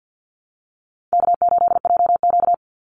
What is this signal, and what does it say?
Morse code, heard as a keyed tone, about 50 wpm, 700 Hz. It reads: K8JQ
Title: Easy Call Signs - 50wpm